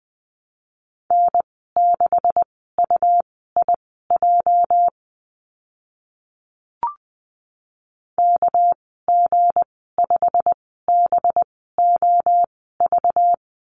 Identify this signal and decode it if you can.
Morse code, heard as a keyed tone, about 20 wpm, spaced as for 15 wpm, 700 Hz. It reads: N6UIJ KG5BOV
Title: 231 Lesson 37g - Letter x - Call Signs 20wpm